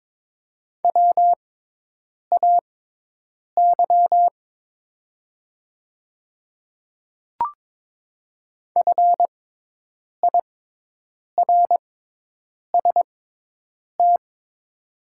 Morse code, heard as a keyed tone, about 22 wpm, spaced as for 8 wpm, 700 Hz. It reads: WAY FIRST
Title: 6x Character Spacing - Top 100 Words - 22wpm